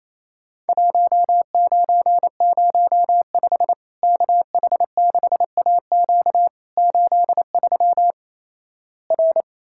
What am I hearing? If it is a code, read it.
Morse code, heard as a keyed tone, about 28 wpm, 700 Hz. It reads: 1905 KH6AQ 83 R
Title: Worked All Europe DX Contest - QTC Exchange - 28wpm